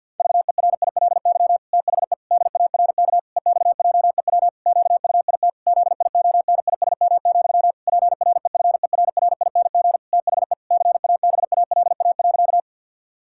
Morse code, heard as a keyed tone, about 50 wpm, 700 Hz. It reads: VERILY THE DARK EXJEW QUIT ZIONISM, PREFERRING THE CABALA.
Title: Pangram Challenge v2 - 50wpm